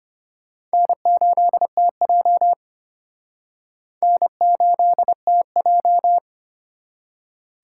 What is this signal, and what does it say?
Morse code, heard as a keyed tone, about 31 wpm, 700 Hz. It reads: N8TJ N8TJ